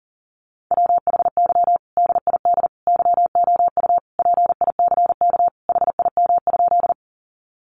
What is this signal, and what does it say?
Morse code, heard as a keyed tone, about 40 wpm, 700 Hz. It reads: WHY DID YOU PICK HIM?
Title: Sentences from Top 1000 Words 40wpm